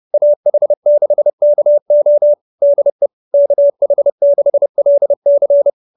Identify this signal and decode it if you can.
Morse code, heard as a keyed tone, about 30 wpm, 590 Hz. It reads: AH6KO DE KH6LC
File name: QSO311 30